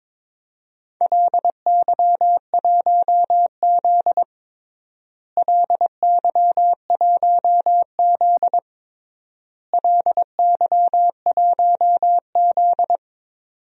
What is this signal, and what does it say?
Morse code, heard as a keyed tone, about 22 wpm, 700 Hz. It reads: LY1Z LY1Z LY1Z
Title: DXCC-100 Call Signs - 4 Characters - Repeated 3x - 22wpm